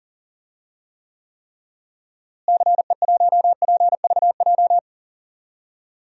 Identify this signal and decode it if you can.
Morse code, heard as a keyed tone, about 40 wpm, 700 Hz. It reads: CE1PVJ
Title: DXCC-100 Call Signs - 6 Characters - Repeated 3x - 40wpm